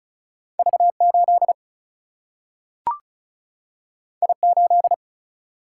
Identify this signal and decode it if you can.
Morse code, heard as a keyed tone, about 36 wpm, 700 Hz. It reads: V 8 I 8
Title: Single-Letter Single-Number 35wpm